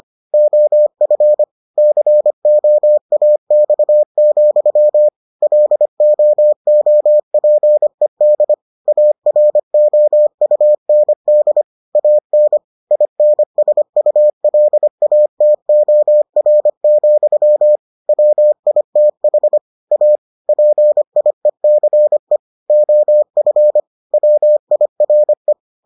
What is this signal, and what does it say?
Morse code, heard as a keyed tone, about 25 wpm, 600 Hz. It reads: OF COAX, LOOPED AROUND AN INSULATOR, WITH A PIECE OF WIRE